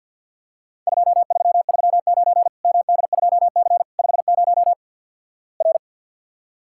Morse code, heard as a keyed tone, about 50 wpm, 700 Hz. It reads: 2339 MD2C 50 R
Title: Worked All Europe DX Contest - QTC Exchange - 50wpm